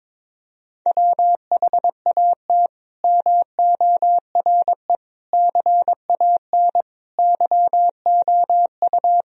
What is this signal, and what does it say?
Morse code, heard as a keyed tone, about 22 wpm, 700 Hz. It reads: WHAT MORE CAN YOU
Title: Sentences from Top 1000 Words - Review Part 4 - 22wpm